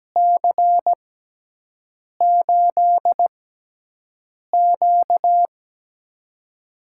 Morse code, heard as a keyed tone, about 17 wpm, spaced as for 6 wpm, 700 Hz. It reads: C8Q
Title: USA Call Signs - 4 Characters - 6x Character Spacing - 17wpm